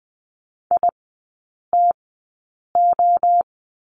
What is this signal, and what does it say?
Morse code, heard as a keyed tone, about 20 wpm, spaced as for 8 wpm, 700 Hz. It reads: ITO
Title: 067 Lesson 12e - Letter d - Three Characters 20wpm